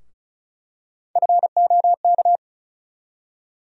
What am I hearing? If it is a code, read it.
Morse code, heard as a keyed tone, about 36 wpm, 700 Hz. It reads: FOK
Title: USA Call Sign Postfixes - 3 Letters 35wpm